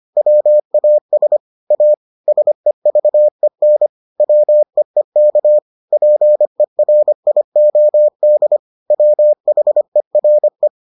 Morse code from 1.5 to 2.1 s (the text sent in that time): A